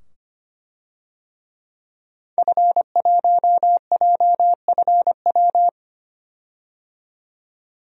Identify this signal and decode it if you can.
Morse code, heard as a keyed tone, about 25 wpm, 700 Hz. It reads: F1JFW
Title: DXCC-100 Call Signs - 5 Characters - 25wpm